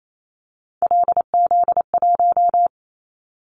Morse code, heard as a keyed tone, about 28 wpm, 700 Hz. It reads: LZ1
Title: Callsign Prefixes for Top 100 DXCC - 28wpm